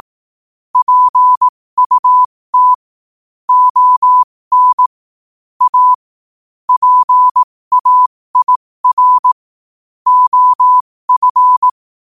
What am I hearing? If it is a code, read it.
Morse code, heard as a keyed tone, about 18 wpm, spaced as for 15 wpm, 1000 Hz. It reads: PUT ON A PAIR OF
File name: qotdfiles / 15wpm